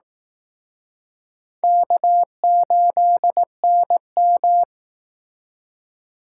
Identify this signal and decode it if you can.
Morse code, heard as a keyed tone, about 19 wpm, 700 Hz. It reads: K8NM